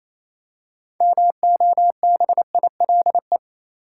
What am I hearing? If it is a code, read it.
Morse code, heard as a keyed tone, about 28 wpm, 700 Hz. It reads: MOBILE
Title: Six Letter Words - 28wpm